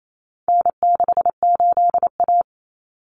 Morse code, heard as a keyed tone, about 28 wpm, 700 Hz. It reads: N68A
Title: Common Cut Numbers - 4-Digit 28wpm